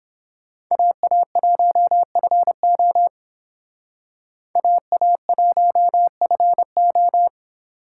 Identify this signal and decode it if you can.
Morse code, heard as a keyed tone, about 28 wpm, 700 Hz. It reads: AA1FO AA1FO